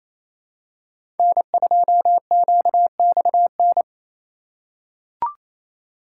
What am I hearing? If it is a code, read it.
Morse code, heard as a keyed tone, about 28 wpm, 700 Hz. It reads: N2QXN
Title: USA Call Signs - 5 Characters - 2x Character Spacing - 28wpm